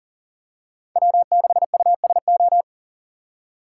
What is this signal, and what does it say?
Morse code, heard as a keyed tone, about 40 wpm, 700 Hz. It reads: W6USO
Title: USA Call Signs - 5 Characters - 2x Character Spacing - 40wpm